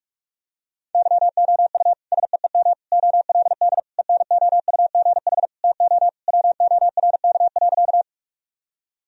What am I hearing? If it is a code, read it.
Morse code, heard as a keyed tone, about 47 wpm, 700 Hz. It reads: YOU SEEM OLD ENOUGH TO WORK.
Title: Sentences from Top 1000 Words - Review Part 1 - 45wpm